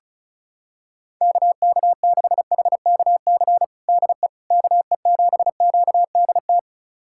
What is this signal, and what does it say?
Morse code, heard as a keyed tone, about 34 wpm, 700 Hz. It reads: KK6HKC DE KE7QDT
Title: USA Callsign DE Callsign - 6 Characters - 35wpm